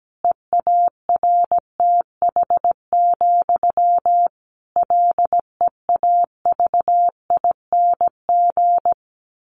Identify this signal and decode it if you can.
Morse code, heard as a keyed tone, about 17 wpm, 700 Hz. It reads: EARTH, LEAVING
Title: A Sound Of Thunder 17wpm